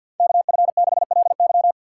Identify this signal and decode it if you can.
Morse code, heard as a keyed tone, about 53 wpm, 700 Hz. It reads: KF6LY
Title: USA Call Signs - 5 Characters - 2x Character Spacing - 50wpm